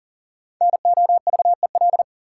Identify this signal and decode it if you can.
Morse code, heard as a keyed tone, about 40 wpm, 700 Hz. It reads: NOVEL
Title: Five Letter Words - 40wpm